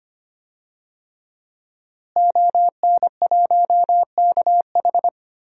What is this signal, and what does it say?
Morse code, heard as a keyed tone, about 25 wpm, 700 Hz. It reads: ON1KH
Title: DXCC-100 Call Signs - 5 Characters - Repeated 3x - 25wpm